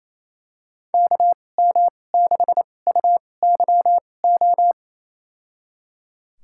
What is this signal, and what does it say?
Morse code, heard as a keyed tone, about 28 wpm, spaced as for 20 wpm, 700 Hz. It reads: KM6UYO
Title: USA Call Signs - 6 Characters - 2x Character Spacing - 28wpm